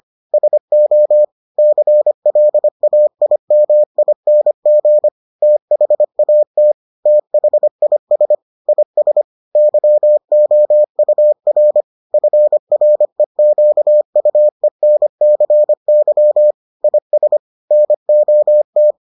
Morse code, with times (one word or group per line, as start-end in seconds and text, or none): 0.3-1.2 SO
1.6-5.1 CLAIMING
5.4-6.7 THAT
7.1-8.3 THIS
8.7-9.2 IS
9.6-11.8 YOUR
12.1-16.5 FREQUENCY
16.8-17.4 IS
17.7-18.9 NOT